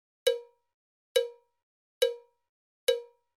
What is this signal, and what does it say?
There is no code.